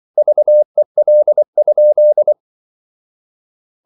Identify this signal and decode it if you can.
Morse code, heard as a keyed tone, about 24 wpm, 600 Hz. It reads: VEL?
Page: Ferie 24 wpm